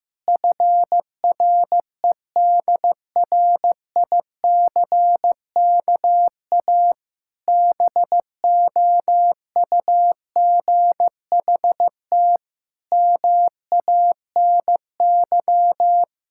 Morse code, heard as a keyed tone, about 15 wpm, 700 Hz. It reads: FREDRICKA BOUGHT MANY
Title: Pangram Challenge v2 - 15wpm